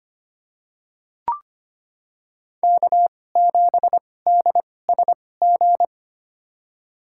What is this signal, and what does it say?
Morse code, heard as a keyed tone, about 25 wpm, spaced as for 18 wpm, 700 Hz. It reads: K7DSG
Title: USA Call Signs - 2x Character Spacing - 25wpm